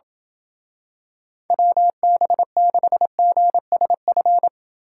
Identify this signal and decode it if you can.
Morse code, heard as a keyed tone, about 27 wpm, 700 Hz. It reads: WB6GSF